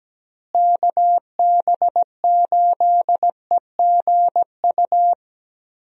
Morse code, heard as a keyed tone, about 17 wpm, 700 Hz. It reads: KB8EGU
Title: USA Call Signs - 6 Characters - Repeated 3x - 17wpm